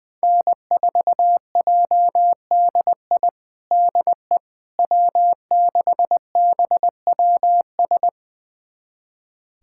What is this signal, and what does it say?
Morse code, heard as a keyed tone, about 20 wpm, 700 Hz. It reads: N4JDI DE W6BWS
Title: USA Callsign DE Callsign - 5 Characters - 20wpm